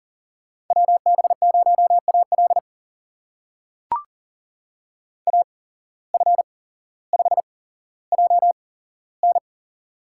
Morse code, heard as a keyed tone, about 40 wpm, spaced as for 17 wpm, 700 Hz. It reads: WB0AL AF5JN
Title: USA Call Signs - 5 Characters - 8x Character Spacing - 40wpm